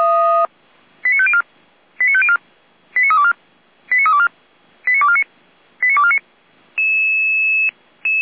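Five-tone selective call (ZVEI), heard as quick runs of five tones, tones of about 80 ms. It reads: 89504 89504 89324 89324 89249 89249
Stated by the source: ZVEI five-tone selective call received on SDR, from the audio server of Feuerwehr Orlamünde.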